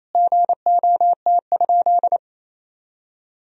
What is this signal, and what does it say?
Morse code, heard as a keyed tone, about 28 wpm, 700 Hz. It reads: GOT?
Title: Sentences from Top 300 Words and Abbreviations 28wpm